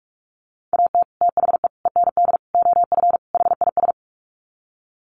Morse code, heard as a keyed tone, about 44 wpm, 700 Hz. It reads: AT THE END OF HIS
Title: Sets of 5 Words Encore 45wpm